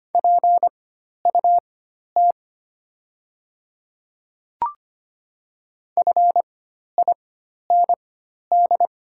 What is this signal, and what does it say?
Morse code, heard as a keyed tone, about 25 wpm, spaced as for 12 wpm, 690 Hz. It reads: PUT FIND
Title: 4x Character Spacing - Top 300 Words - 25wpm